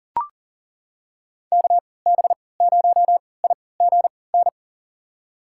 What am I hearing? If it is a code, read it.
Morse code, heard as a keyed tone, about 40 wpm, spaced as for 23 wpm, 700 Hz. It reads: KB0IGN
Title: USA Call Signs - 6 Characters - 3x Character Spacing - 40wpm